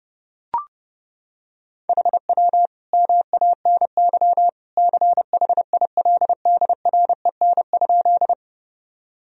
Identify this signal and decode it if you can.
Morse code, heard as a keyed tone, about 30 wpm, 700 Hz. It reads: HW MANY CHILDREN?
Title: Sentences from Top 300 Words and Abbreviations 30wpm